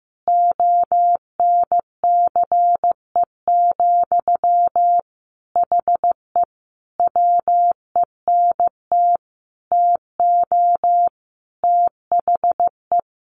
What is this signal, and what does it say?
Morse code, heard as a keyed tone, about 14 wpm, 700 Hz. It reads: ONCE, HE WENT TO THE
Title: Sentences from Top 1300 Words 15wpm